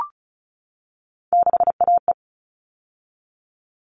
Morse code, heard as a keyed tone, about 34 wpm, 700 Hz. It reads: 6AE
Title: All Cut Numbers - 3-Digit 35wpm